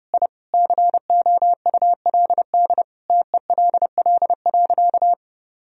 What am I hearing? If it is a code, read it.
Morse code, heard as a keyed tone, about 30 wpm, 700 Hz. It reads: I COULD TELL.
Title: Sentences from Top 1000 Words - Review Part 4 - 30wpm